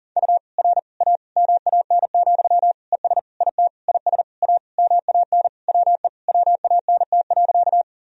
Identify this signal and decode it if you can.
Morse code, heard as a keyed tone, about 40 wpm, 700 Hz. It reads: U R A MAN, ES IT IS A MAN WE WANT.
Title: Sentences from Top 200 Words and Abbreviations 40wpm